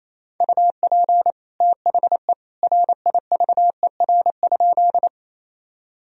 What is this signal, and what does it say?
Morse code, heard as a keyed tone, about 28 wpm, 700 Hz. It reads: UP THE RIVER?
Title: Sentences from Top 1000 Words - Review Part 1 - 28wpm